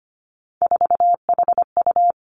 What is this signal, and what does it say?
Morse code, heard as a keyed tone, about 25 wpm, 700 Hz. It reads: 4HU
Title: Three Challenging Letter-Number Combinations 25wpm